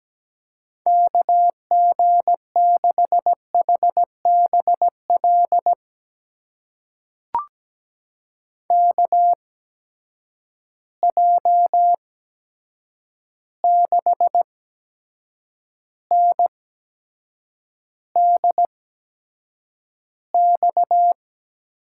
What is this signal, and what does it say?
Morse code, heard as a keyed tone, about 17 wpm, spaced as for 7 wpm, 700 Hz. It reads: KG6HBL KJ6NDX
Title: USA Call Signs - 6 Characters - 8x Character Spacing - 17wpm